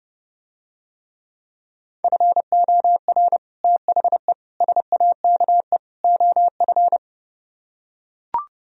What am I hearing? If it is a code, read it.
Morse code, heard as a keyed tone, about 32 wpm, 700 Hz. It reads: FOR THE SAKE OF